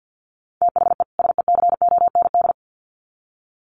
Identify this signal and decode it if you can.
Morse code, heard as a keyed tone, about 50 wpm, 700 Hz. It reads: THE SECOND
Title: Sets of 2 Words 50wpm